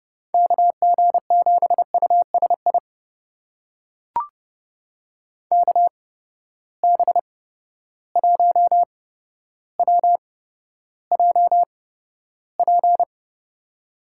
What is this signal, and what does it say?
Morse code, heard as a keyed tone, about 32 wpm, spaced as for 13 wpm, 700 Hz. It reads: KG7USI KB1WJP
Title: USA Call Signs - 6 Characters - 8x Character Spacing - 30wpm